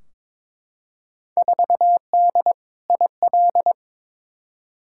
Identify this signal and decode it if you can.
Morse code, heard as a keyed tone, about 22 wpm, 700 Hz. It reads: 4D IL
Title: Field Day Exchanges Encore - 22wpm